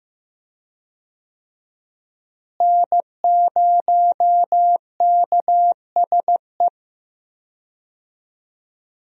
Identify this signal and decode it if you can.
Morse code, heard as a keyed tone, about 16 wpm, 700 Hz. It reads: N0KSE
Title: USA Call Signs - 5 Characters - Repeated 2x - 15wpm